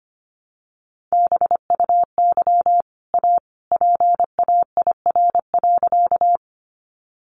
Morse code, heard as a keyed tone, about 25 wpm, 700 Hz. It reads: BUY A PAIR.
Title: Sentences from Top 1200 Words 25wpm